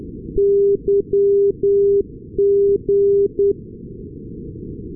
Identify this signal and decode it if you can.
Morse code, heard as a keyed tone, about 10 wpm, 400 Hz. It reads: YG
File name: europe / nor / nor-258 yg-396 la6lu-2020-12-kiwi regional-rb